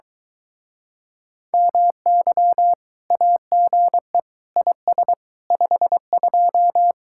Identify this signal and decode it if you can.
Morse code, heard as a keyed tone, about 23 wpm, 700 Hz. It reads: MY AGE IS 52